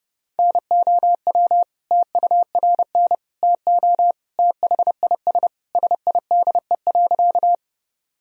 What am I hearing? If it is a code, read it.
Morse code, heard as a keyed tone, about 30 wpm, 700 Hz. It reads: NOW TURN TO THIS SIDE.
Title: Sentences from Top 1000 Words - Review Part 1 - 30wpm